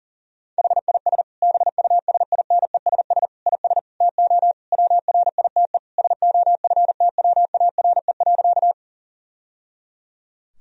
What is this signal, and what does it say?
Morse code, heard as a keyed tone, about 40 wpm, 700 Hz. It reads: HIS BUSINESS IS TO WRITE SOFTWARE.
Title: Sentences from Top 1000 Words - Review Part 3 - 40wpm